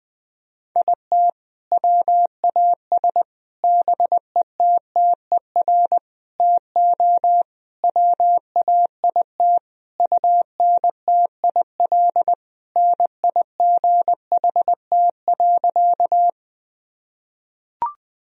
Morse code, heard as a keyed tone, about 20 wpm, 700 Hz. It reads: IT WAS BETTER TO WAIT UNTIL NIGHT.
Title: Sentences from Top 1000 Words - Review Part 1 - 20wpm